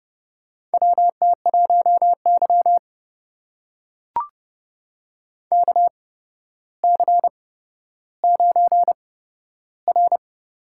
Morse code, heard as a keyed tone, about 30 wpm, spaced as for 13 wpm, 700 Hz. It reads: WT1Y KC9R
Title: USA Call Signs - 8x Character Spacing - 30wpm